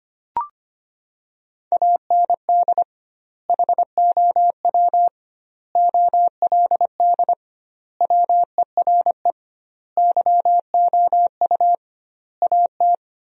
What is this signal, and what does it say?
Morse code, heard as a keyed tone, about 25 wpm, 700 Hz. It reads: AND HOW OLD WERE YOU AT
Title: 2x Word-Spacing - Sentences from Top 200 Words - 25wpm